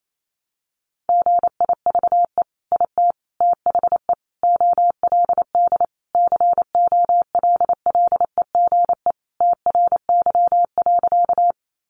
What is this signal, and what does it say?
Morse code, heard as a keyed tone, about 28 wpm, 700 Hz. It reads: GIVE IT THE OLD COLLEGE TRY.
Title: Sentences from Top 1300 Words 28wpm